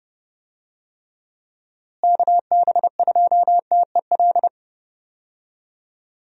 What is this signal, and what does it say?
Morse code, heard as a keyed tone, about 30 wpm, 700 Hz. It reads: KB2TEL
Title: USA Call Signs - 6 Characters - Repeated 3x - 30wpm